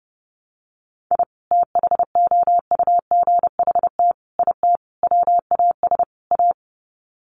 Morse code, heard as a keyed tone, about 29 wpm, 700 Hz. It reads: I THOUGHT IT WAS A
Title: Sets of 5 Words 30wpm